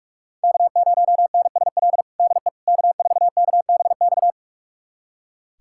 Morse code, heard as a keyed tone, about 46 wpm, 700 Hz. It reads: K0NSL DE K4KBX
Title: USA Callsign DE Callsign - 5 Characters - 45wpm